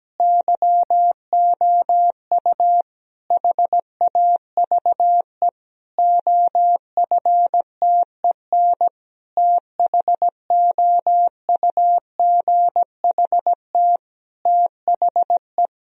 Morse code, heard as a keyed tone, about 17 wpm, 700 Hz. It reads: YOU HAVE OFTEN THOUGHT THE
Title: Sentences from Top 1000 Words - Review Part 1 - 17wpm